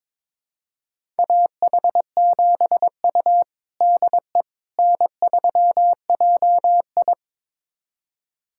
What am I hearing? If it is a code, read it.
Morse code, heard as a keyed tone, about 21 wpm, 700 Hz. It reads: AH7U DE N3JI
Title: USA Callsign DE Callsign - 4 Characters - 22wpm